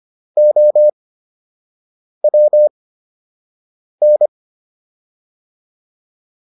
Morse code, heard as a keyed tone, about 25 wpm, spaced as for 6 wpm, 600 Hz. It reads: OWN